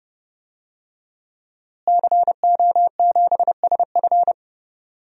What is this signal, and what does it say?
Morse code, heard as a keyed tone, about 30 wpm, 700 Hz. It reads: CO7SF
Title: DXCC-100 Call Signs - 5 Characters - Repeated 2x - 30wpm